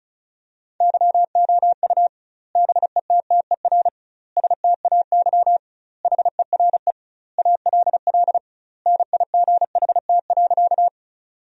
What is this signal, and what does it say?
Morse code, heard as a keyed tone, about 34 wpm, 700 Hz. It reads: YOU BETTER STAY HERE ALL NIGHT.
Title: 2x Word-Spacing - Sentences from Top 500 Words - 35wpm